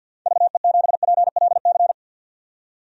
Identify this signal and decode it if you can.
Morse code, heard as a keyed tone, about 51 wpm, 700 Hz. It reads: VE7PLC
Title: DXCC-100 Call Signs - 6 Characters - 50wpm